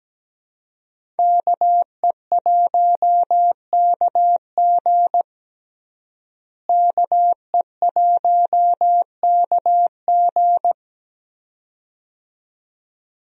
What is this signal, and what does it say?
Morse code, heard as a keyed tone, about 17 wpm, 700 Hz. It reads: KE1KG KE1KG
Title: USA Call Signs - 5 Characters - Repeated 2x - 17wpm